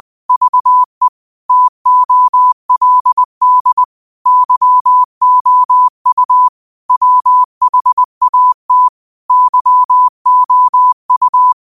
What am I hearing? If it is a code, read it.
Morse code, heard as a keyed tone, about 20 wpm, 1000 Hz. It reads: VE TOLD YOU WHAT YOU
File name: qotdfiles / 20wpm